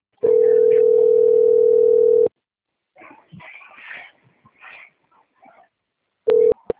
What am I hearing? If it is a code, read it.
Ringback tone (North America), 440 + 480 Hz.
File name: out- - - - - .54896